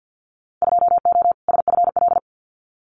Morse code, heard as a keyed tone, about 50 wpm, 700 Hz. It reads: 2O SFL
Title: Winter Field Day Exchanges 50wpm